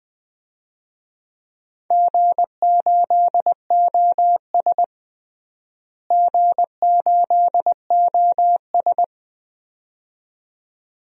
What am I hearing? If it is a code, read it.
Morse code, heard as a keyed tone, about 21 wpm, 700 Hz. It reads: G8OS G8OS
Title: DXCC-100 Call Signs - 4 Characters - Repeated 2x - 20wpm